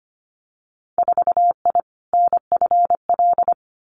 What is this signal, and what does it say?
Morse code, heard as a keyed tone, about 25 wpm, 690 Hz. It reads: 4I NFL
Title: Winter Field Day Exchanges Encore 25wpm